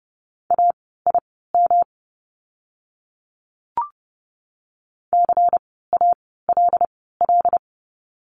Morse code, heard as a keyed tone, about 30 wpm, spaced as for 17 wpm, 700 Hz. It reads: AIM CALL
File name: course / 30:10wpm / 108 Lesson 18f - Letter m - Words 30wpm